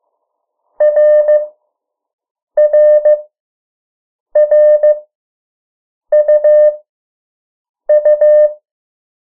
Morse code, heard as a keyed tone, about 15 wpm, spaced as for 6 wpm, 600 Hz. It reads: RRRUU
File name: morse mp3 / 03 urok 2